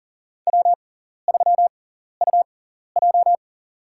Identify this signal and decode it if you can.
Morse code, heard as a keyed tone, about 39 wpm, spaced as for 14 wpm, 700 Hz. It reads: W3UJ
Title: USA Call Signs - 4 Characters - 6x Character Spacing - 40wpm